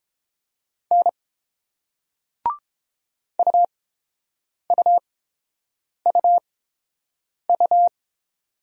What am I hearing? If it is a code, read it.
Morse code, heard as a keyed tone, about 30 wpm, spaced as for 7 wpm, 690 Hz. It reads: N UUUU